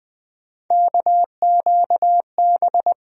Morse code, heard as a keyed tone, about 20 wpm, 700 Hz. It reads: KQB
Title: USA Call Sign Postfixes - 3 Letters 20wpm